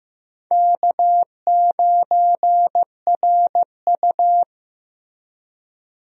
Morse code, heard as a keyed tone, about 15 wpm, 700 Hz. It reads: K9RU
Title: USA Call Signs - 4 Characters - Repeated 2x - 15wpm